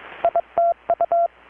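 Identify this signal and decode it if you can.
Morse code, heard as a keyed tone, about 23 wpm, 680 Hz. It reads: ITU
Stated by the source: Morse from KPH's Night of Nights XX maritime radiotelegraphy event, received in Chiapas, Mexico about 3,800 km away on a Kenwood R-600 with a grounded 10 m long-wire antenna.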